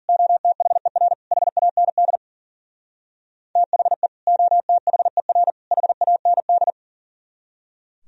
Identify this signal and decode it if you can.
Morse code, heard as a keyed tone, about 49 wpm, 700 Hz. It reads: OTHER HAND THE OTHER HAND
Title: Speed-Racing - Sets of 3 Words 40wpm